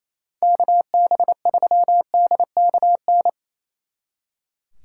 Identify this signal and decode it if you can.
Morse code, heard as a keyed tone, about 34 wpm, 700 Hz. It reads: KB3DKN
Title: Speed Racing - USA 2x3 Call Signs 28wpm